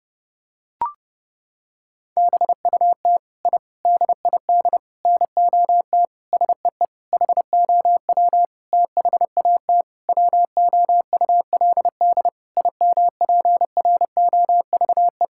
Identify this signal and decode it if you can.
Morse code, heard as a keyed tone, about 30 wpm, 700 Hz. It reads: BUT I DID NOT SEE HOW THAT WOULD IMPROVE
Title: Sentences from Top 1000 Words - Review Part 5 - 30wpm